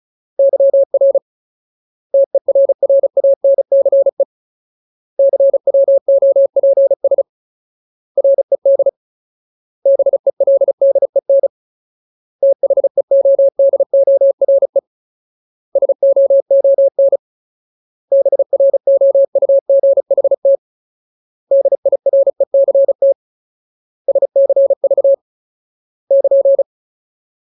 Morse code, heard as a keyed tone, about 34 wpm, 550 Hz. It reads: YR TERRANCE CWOPS RED BELDEN THEODORE SOON BROUGHT DIRECT SCV <KN>
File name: Ultimate Word List / 35WPM Ultimate